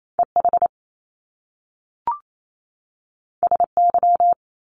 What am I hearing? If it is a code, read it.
Morse code, heard as a keyed tone, about 28 wpm, 700 Hz. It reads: EH SY
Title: Top Two Letter Combinations in Words 28wpm